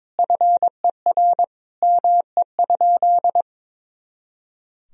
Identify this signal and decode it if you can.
Morse code, heard as a keyed tone, about 22 wpm, 700 Hz. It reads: FER ME?
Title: Sentences from Top 500 Words and Abbreviations 22wpm